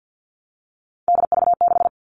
Morse code, heard as a keyed tone, about 48 wpm, 700 Hz. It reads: DV6
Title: All Cut Numbers - 3-Digit 50wpm